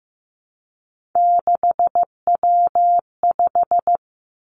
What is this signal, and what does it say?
Morse code, heard as a keyed tone, about 15 wpm, 700 Hz. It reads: 6W5
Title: Three Challenging Letter-Number Combinations 15wpm